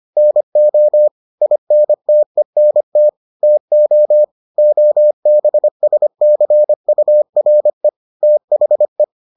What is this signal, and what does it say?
Morse code, heard as a keyed tone, about 25 wpm, 600 Hz. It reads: NO INTENT TO OBSCURE THE